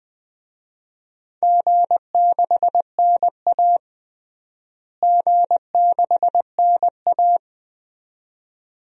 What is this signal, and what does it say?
Morse code, heard as a keyed tone, about 20 wpm, 700 Hz. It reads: G6NA G6NA
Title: DXCC-100 Call Signs - 4 Characters - Repeated 2x - 20wpm